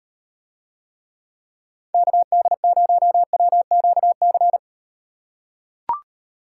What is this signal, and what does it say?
Morse code, heard as a keyed tone, about 38 wpm, 700 Hz. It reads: KD0WQC